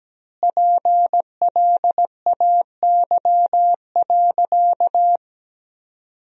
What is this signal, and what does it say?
Morse code, heard as a keyed tone, about 17 wpm, 700 Hz. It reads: PLAY.
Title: 3x Word-Spacing - Sentences from Top 200 Words - 17wpm